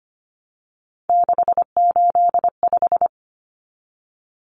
Morse code, heard as a keyed tone, about 25 wpm, 700 Hz. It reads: 685
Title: Numbers - 3-Digit 25wpm